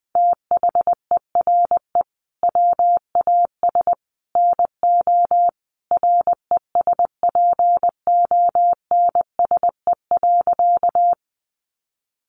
Morse code, heard as a keyed tone, about 20 wpm, 700 Hz. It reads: THERE WAS NO RESPONSE.
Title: Sentences from Top 1400 Words 20wpm